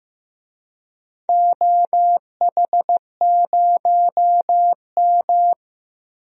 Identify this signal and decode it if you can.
Morse code, heard as a keyed tone, about 15 wpm, 700 Hz. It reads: OH0M
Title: DXCC-100 Call Signs - 4 Characters - Repeated 3x - 15wpm